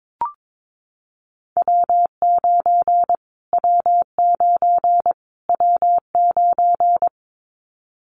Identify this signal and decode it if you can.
Morse code, heard as a keyed tone, about 22 wpm, 700 Hz. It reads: W9 W9 W9
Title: Callsign Prefixes for Top 25 DXCC - Repeated 3x - 22wpm